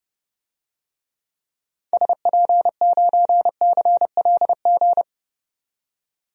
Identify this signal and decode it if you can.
Morse code, heard as a keyed tone, about 32 wpm, 700 Hz. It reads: SP9CLG
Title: DXCC-100 Call Signs - 6 Characters - 30wpm